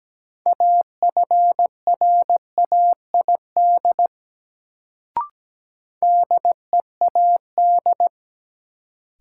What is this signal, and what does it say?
Morse code, heard as a keyed tone, about 17 wpm, 700 Hz. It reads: AFRAID DEAD
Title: Sentences from Top 1500 Words - New Word Drill - 17wpm